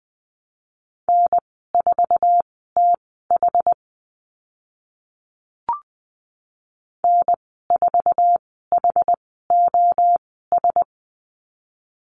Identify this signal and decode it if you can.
Morse code, heard as a keyed tone, about 20 wpm, spaced as for 15 wpm, 700 Hz. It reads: N4TH N4HOS
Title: 088 Lesson 15g - Number 5 - Call Signs 20wpm